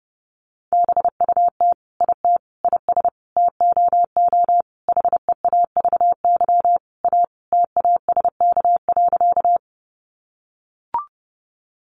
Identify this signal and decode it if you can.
Morse code, heard as a keyed tone, about 30 wpm, 700 Hz. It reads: BUT IT IS TOO HEAVY A TASK.
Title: Sentences from Top 1100 Words 30wpm